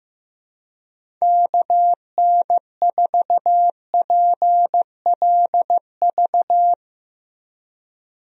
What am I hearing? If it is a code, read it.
Morse code, heard as a keyed tone, about 16 wpm, 700 Hz. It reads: KN4PLV
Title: USA Call Signs - 6 Characters - Repeated 3x - 15wpm